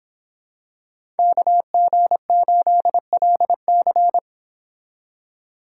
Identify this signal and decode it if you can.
Morse code, heard as a keyed tone, about 25 wpm, 700 Hz. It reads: KG8LC